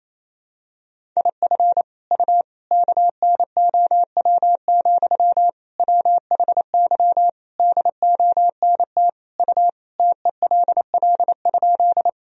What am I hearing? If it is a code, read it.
Morse code, heard as a keyed tone, about 28 wpm, 700 Hz. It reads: IF U KNOW, WHY DONT U TELL?
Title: Sentences from Top 300 Words and Abbreviations 28wpm